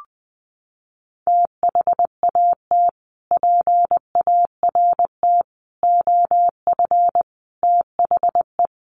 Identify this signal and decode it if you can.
Morse code, heard as a keyed tone, about 20 wpm, 700 Hz. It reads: THAT PART OF THE
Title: Sets of 5 Words Encore2 20wpm